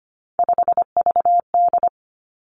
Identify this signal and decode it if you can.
Morse code, heard as a keyed tone, about 25 wpm, 700 Hz. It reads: 5VD
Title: Three Challenging Letter-Number Combinations 25wpm